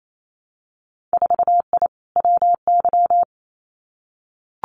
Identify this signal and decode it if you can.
Morse code, heard as a keyed tone, about 28 wpm, 700 Hz. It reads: 4I WY
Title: Winter Field Day Exchanges Encore 28wpm